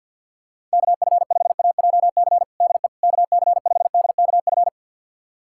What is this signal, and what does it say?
Morse code, heard as a keyed tone, about 48 wpm, 700 Hz. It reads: KF5AJC DE KC5DKF
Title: USA Callsign DE Callsign - 6 Characters - 50wpm